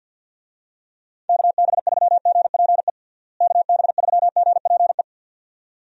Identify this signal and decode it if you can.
Morse code, heard as a keyed tone, about 51 wpm, 700 Hz. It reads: KB3GPE KB3GPE
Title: USA Call Signs - 6 Characters - Repeated 2x - 50wpm